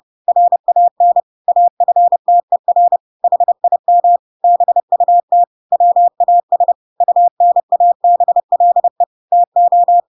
Morse code, heard as a keyed tone, about 30 wpm, 700 Hz. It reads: RAN AFTER HIM BUT WAS UNABLE TO
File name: Pinocchio / Pinocchio ch3 30 30 ik1hge